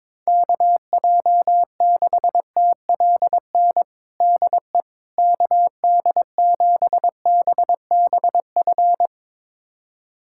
Morse code, heard as a keyed tone, about 22 wpm, 700 Hz. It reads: KJ6TLN DE KD7BBF
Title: USA Callsign DE Callsign - 6 Characters - 22wpm